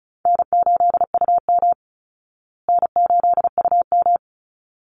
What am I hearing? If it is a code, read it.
Morse code, heard as a keyed tone, about 34 wpm, 700 Hz. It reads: N8UM N8UM
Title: Call Signs Repeated 3x - 35wpm